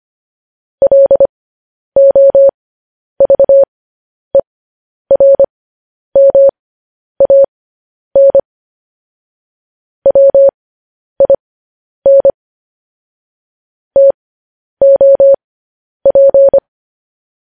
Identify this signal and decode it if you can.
Morse code, heard as a keyed tone, about 25 wpm, spaced as for 10 wpm, 550 Hz. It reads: LOVERMAN WIN TOP
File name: podcast / News headlines-1x-25x10